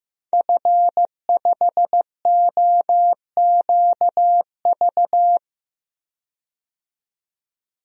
Morse code, heard as a keyed tone, about 15 wpm, 700 Hz. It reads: F5OQV
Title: DXCC-100 Call Signs - 5 Characters - 15wpm